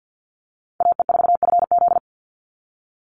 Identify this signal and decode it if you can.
Morse code, heard as a keyed tone, about 49 wpm, 700 Hz. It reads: AE4FZ
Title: NA SOTA Chaser Callsigns 50wpm